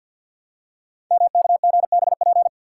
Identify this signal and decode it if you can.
Morse code, heard as a keyed tone, about 52 wpm, 700 Hz. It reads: MKGBP
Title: Code Groups - 5 Letters - 50wpm